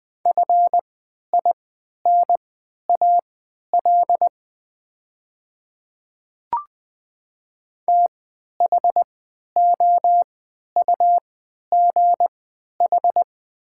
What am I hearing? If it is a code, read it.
Morse code, heard as a keyed tone, about 20 wpm, spaced as for 11 wpm, 700 Hz. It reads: FINAL THOUGH
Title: 3x Character Spacing - Top 500 Words - 20wpm